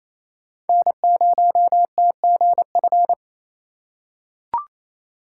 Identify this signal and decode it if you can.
Morse code, heard as a keyed tone, about 28 wpm, 700 Hz. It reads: N0TGF
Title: USA Call Signs - 4x Character Spacing - 28wpm